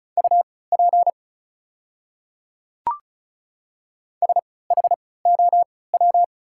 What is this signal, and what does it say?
Morse code, heard as a keyed tone, about 36 wpm, spaced as for 20 wpm, 700 Hz. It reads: UP SHOW
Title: 3x Character Spacing - Top 200 Words - 35wpm